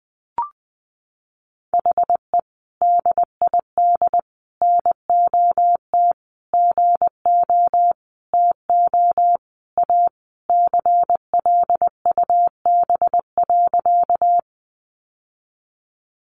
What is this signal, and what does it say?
Morse code, heard as a keyed tone, about 20 wpm, 700 Hz. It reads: HE DID NOT GO TO A CLUB.
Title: Sentences from Top 1200 Words 20wpm